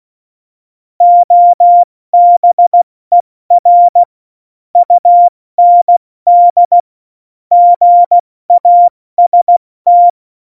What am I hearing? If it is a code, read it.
Morse code, heard as a keyed tone, about 16 wpm, 700 Hz. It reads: OBER UND GAST
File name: Morsekurs / Witze / Witz 22. Oktober 2019, 70 F80